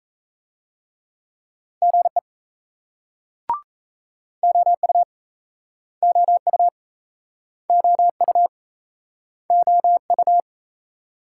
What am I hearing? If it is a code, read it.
Morse code, heard as a keyed tone, about 39 wpm, 700 Hz. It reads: GE OU OU OU OU